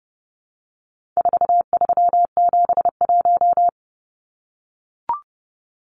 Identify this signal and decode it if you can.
Morse code, heard as a keyed tone, about 30 wpm, 700 Hz. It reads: 4371
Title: Numbers - 4-Digit 30wpm